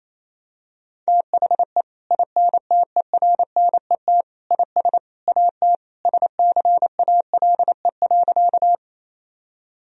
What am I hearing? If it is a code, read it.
Morse code, heard as a keyed tone, about 28 wpm, 700 Hz. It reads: THE INTERNET IS AT SCALE.
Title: Sentences from Top 1000 Words - Review Part 4 - 28wpm